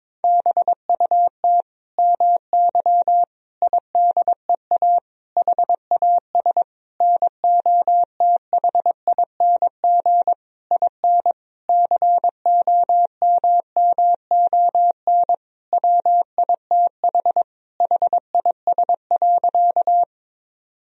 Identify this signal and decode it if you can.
Morse code, heard as a keyed tone, about 22 wpm, 700 Hz. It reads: BUT MY IDEA HAS NOTHING IN COMMON WITH HIS.
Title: Sentences from Top 1000 Words - Review Part 3 - 22wpm